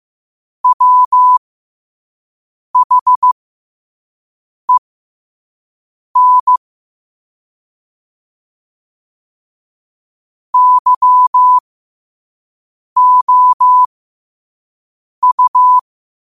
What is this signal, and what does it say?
Morse code, heard as a keyed tone, about 15 wpm, spaced as for 5 wpm, 1000 Hz. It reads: WHEN YOU
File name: 5wpm